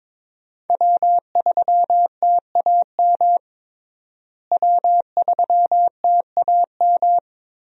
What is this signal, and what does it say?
Morse code, heard as a keyed tone, about 22 wpm, 700 Hz. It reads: W3TAM W3TAM
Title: USA Call Signs - 5 Characters - Repeated 3x - 22wpm